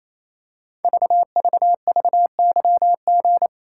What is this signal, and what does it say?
Morse code, heard as a keyed tone, about 27 wpm, 690 Hz. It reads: VVVYG
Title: Code Groups - 5 Letters - 28wpm